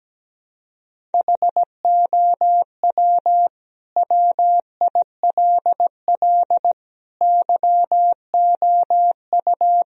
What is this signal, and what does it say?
Morse code, heard as a keyed tone, about 17 wpm, 700 Hz. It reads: HOW WILL YOU
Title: Sentences from Top 1000 Words - Review Part 2 - 17wpm